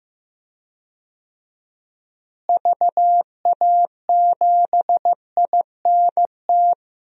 Morse code, heard as a keyed tone, about 15 wpm, 700 Hz. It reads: VA7INT
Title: DXCC-100 Call Signs - 6 Characters - Repeated 3x - 15wpm